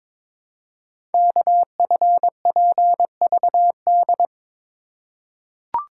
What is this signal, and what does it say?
Morse code, heard as a keyed tone, about 23 wpm, 700 Hz. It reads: KFPVD
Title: Code Groups - 5 Letters - 22wpm